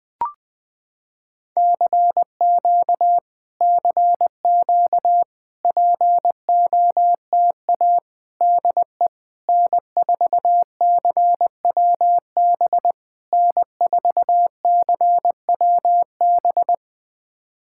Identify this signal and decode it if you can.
Morse code, heard as a keyed tone, about 20 wpm, 700 Hz. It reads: CQ CQ POTA DE N4CWB N4CWB
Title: POTA QSO Practice - 20wpm